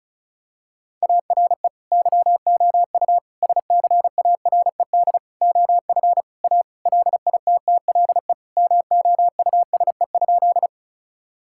Morse code, heard as a keyed tone, about 34 wpm, 700 Hz. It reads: ARE YOU SCARED OF A LITTLE MOUSE?
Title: Sentences from Top 1000 Words - Review Part 5 - 35wpm